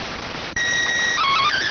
SSTV (slow-scan television), PD 290.